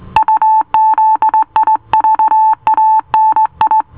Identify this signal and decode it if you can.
Morse code, heard as a keyed tone, about 18 wpm, 880 Hz. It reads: UZIVANI